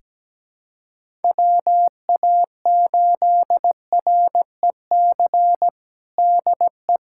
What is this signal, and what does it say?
Morse code, heard as a keyed tone, about 17 wpm, 700 Hz. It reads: WA8REC DE
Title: USA Callsign DE Callsign - 6 Characters - 17wpm